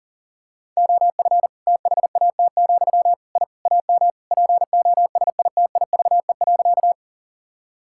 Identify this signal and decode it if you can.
Morse code, heard as a keyed tone, about 40 wpm, 700 Hz. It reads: OF THAT, I AM POSITIVE.
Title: Sentences from Top 1000 Words - Review Part 5 - 40wpm